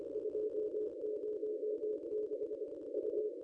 Morse code, heard as a keyed tone, about 24 wpm, 440 Hz. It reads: 206 U
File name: contests / ss-cw-k5tr-op-n5rz -1600